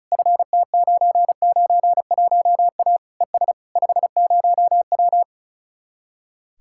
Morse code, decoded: FT991A ES 50W

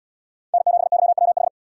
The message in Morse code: N7YWH

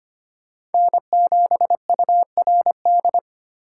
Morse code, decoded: N7URD

